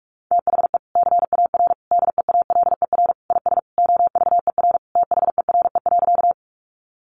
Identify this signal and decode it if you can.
Morse code, heard as a keyed tone, about 44 wpm, 700 Hz. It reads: THE CAR DEALER IS OVER THERE.
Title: Sentences from Top 1100 Words 45wpm